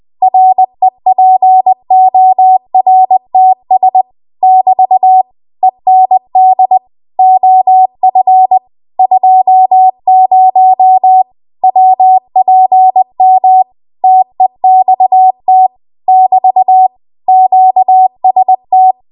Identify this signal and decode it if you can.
Morse code, heard as a keyed tone, about 20 wpm, 750 Hz. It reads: REPORTS <BT> END OF 20 WPM TEXT <BT> QST
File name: Archive / 20 WPM / 20WPM